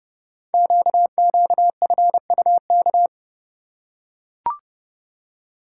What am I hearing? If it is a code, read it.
Morse code, heard as a keyed tone, about 30 wpm, 700 Hz. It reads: QQFUK
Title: Code Groups - 5 Letters - 30wpm